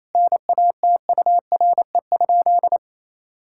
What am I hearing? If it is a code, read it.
Morse code, heard as a keyed tone, about 28 wpm, 700 Hz. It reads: NATURE?
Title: Sentences from Top 1000 Words - Review Part 5 - 28wpm